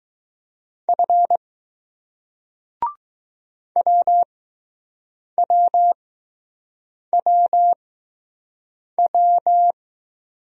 Morse code, heard as a keyed tone, about 20 wpm, spaced as for 6 wpm, 700 Hz. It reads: F WWWW